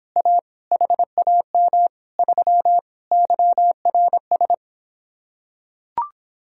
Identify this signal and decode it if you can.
Morse code, heard as a keyed tone, about 26 wpm, 700 Hz. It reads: A HAM 3 YRS